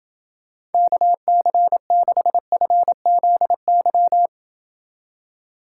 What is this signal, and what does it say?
Morse code, heard as a keyed tone, about 29 wpm, 700 Hz. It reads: KC6FZY